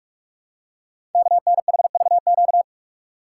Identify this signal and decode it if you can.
Morse code, heard as a keyed tone, about 44 wpm, 700 Hz. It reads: KNHVQ
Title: Code Groups - 5 Letters - 45wpm